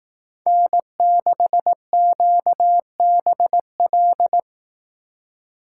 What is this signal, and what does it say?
Morse code, heard as a keyed tone, about 18 wpm, 700 Hz. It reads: N6QBL
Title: Speed Racing - USA 1x3 Call Signs 15wpm